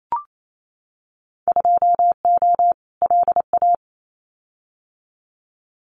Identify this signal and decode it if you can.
Morse code, heard as a keyed tone, about 28 wpm, 700 Hz. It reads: 2O LA
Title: Winter Field Day Exchanges 28wpm